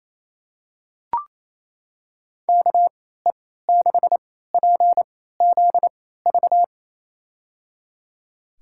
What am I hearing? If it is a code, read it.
Morse code, heard as a keyed tone, about 28 wpm, spaced as for 16 wpm, 700 Hz. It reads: KE6PZV
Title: USA Call Signs - 3x Character Spacing - 28wpm